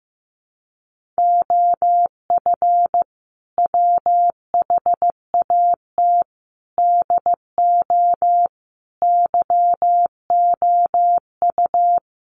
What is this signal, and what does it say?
Morse code, heard as a keyed tone, about 15 wpm, 700 Hz. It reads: OF WHAT DO YOU
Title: Sentences from Top 1300 Words 15wpm